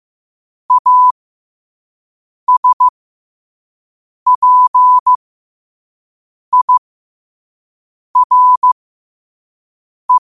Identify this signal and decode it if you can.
Morse code, heard as a keyed tone, about 15 wpm, spaced as for 5 wpm, 1000 Hz. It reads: ASPIRE
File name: 5wpm